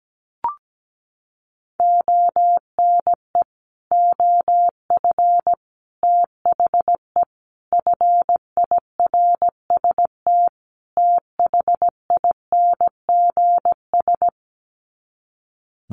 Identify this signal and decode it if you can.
Morse code, heard as a keyed tone, about 17 wpm, 700 Hz. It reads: ONE OF THE FIRST THINGS
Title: Sets of 5 Words Encore 17wpm